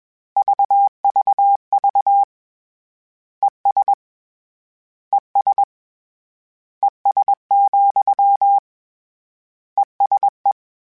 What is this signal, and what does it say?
Morse code, heard as a keyed tone, about 21 wpm, 800 Hz. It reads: VVV ES ES ES, ESE